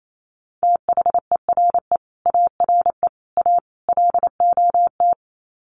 Morse code, heard as a keyed tone, about 28 wpm, 700 Hz. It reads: THERE ARE A LOT
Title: Sets of 4 Words 28wpm